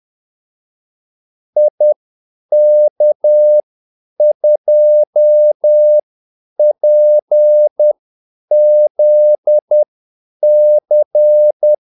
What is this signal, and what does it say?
Morse code, heard as a keyed tone, about 10 wpm, spaced as for 8 wpm, 600 Hz. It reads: IK2PZC
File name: Callsigns-10-8wpm